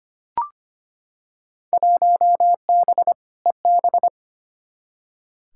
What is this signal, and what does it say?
Morse code, heard as a keyed tone, about 25 wpm, 700 Hz. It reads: 1B EB
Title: Field Day Exchanges - 25wpm